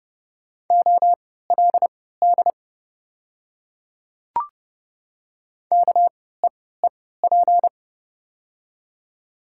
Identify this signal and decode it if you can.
Morse code, heard as a keyed tone, about 30 wpm, spaced as for 17 wpm, 690 Hz. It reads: OLD KEEP
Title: 3x Character Spacing - Top 500 Words - 30wpm